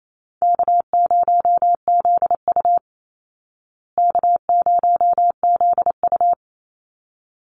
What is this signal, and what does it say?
Morse code, heard as a keyed tone, about 28 wpm, 700 Hz. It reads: K0ZU K0ZU
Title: Easy Call Signs Repeated 2x - 28wpm